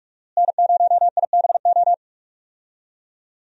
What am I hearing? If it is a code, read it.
Morse code, heard as a keyed tone, about 46 wpm, 700 Hz. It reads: N0IBO
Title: USA Call Signs - 5 Characters - Repeated 3x - 45wpm